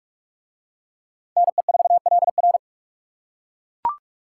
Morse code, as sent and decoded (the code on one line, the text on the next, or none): -. ....- .-.. .-.
NE4LR